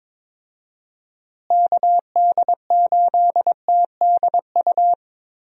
Morse code, heard as a keyed tone, about 21 wpm, 700 Hz. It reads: KD8TDU
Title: USA Call Signs - 6 Characters - Repeated 3x - 22wpm